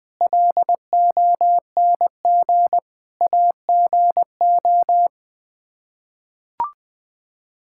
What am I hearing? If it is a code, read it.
Morse code, heard as a keyed tone, about 21 wpm, 700 Hz. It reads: LONG AGO
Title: Sets of 8 Words - 20wpm